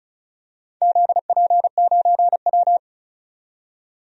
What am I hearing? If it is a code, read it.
Morse code, heard as a keyed tone, about 36 wpm, 700 Hz. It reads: ZP9W
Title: DXCC-100 Call Signs - 4 Characters - 35wpm